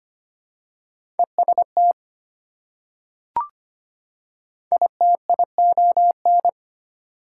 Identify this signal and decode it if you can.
Morse code, heard as a keyed tone, about 25 wpm, 700 Hz. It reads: EST ITION
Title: Prefixes and Suffixes 25wpm